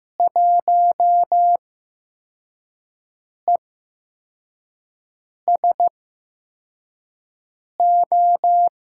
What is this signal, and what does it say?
Morse code, heard as a keyed tone, about 15 wpm, spaced as for 4 wpm, 700 Hz. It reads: 1ESO
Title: USA Call Signs - 8x Character Spacing - 15wpm